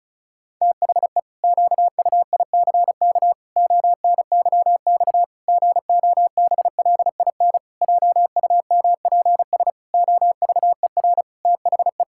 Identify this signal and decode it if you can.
Morse code, heard as a keyed tone, about 34 wpm, 700 Hz. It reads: THE QUICK ONYX GOBLIN JUMPS OVER THE
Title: Pangram Challenge v2 - 35wpm